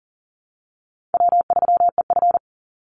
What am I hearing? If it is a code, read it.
Morse code, heard as a keyed tone, about 40 wpm, 700 Hz. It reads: W3EF
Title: Easy Call Signs - 40wpm